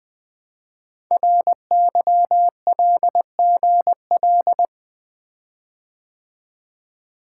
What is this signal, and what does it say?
Morse code, heard as a keyed tone, about 20 wpm, 700 Hz. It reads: RYLGL
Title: Code Groups - 5 Letters - 20wpm